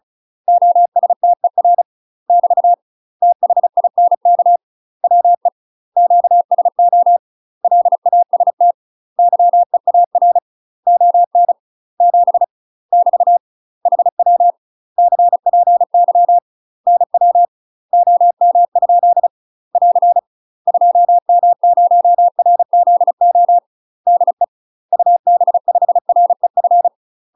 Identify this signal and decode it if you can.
Morse code, heard as a keyed tone, about 34 wpm, 700 Hz. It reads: OSTER <BT> THINK WE QSO LAST YEAR ON 7 <BT> HW CPY NW OM? <AR> 2M0RZO DE UB5REF
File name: UK MASTER QSOs TRAINING / 35WPM / UK MASTER QSOs 0001